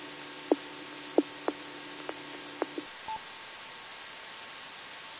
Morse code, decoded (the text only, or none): none